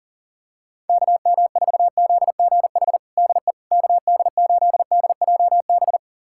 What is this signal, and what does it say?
Morse code, heard as a keyed tone, about 40 wpm, 700 Hz. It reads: KM4ZGH DE KD8DJB
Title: USA Callsign DE Callsign 40wpm